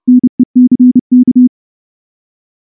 Morse code, heard as a keyed tone, about 28 wpm, 250 Hz. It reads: NECK
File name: morse-training-30WPM